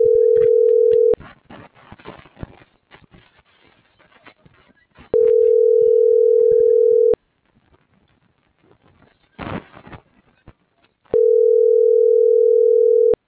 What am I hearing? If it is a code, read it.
Ringback tone (North America), 440 + 480 Hz.